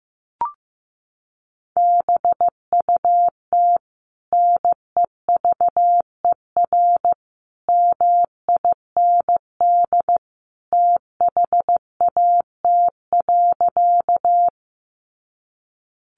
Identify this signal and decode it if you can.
Morse code, heard as a keyed tone, about 15 wpm, 700 Hz. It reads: BUT NEVER MIND THAT.
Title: Sentences from Top 500 Words 15wpm